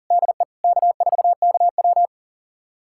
Morse code, heard as a keyed tone, about 40 wpm, 700 Hz. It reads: DE K4KW
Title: USA Callsign DE Callsign - 4 Characters - 40wpm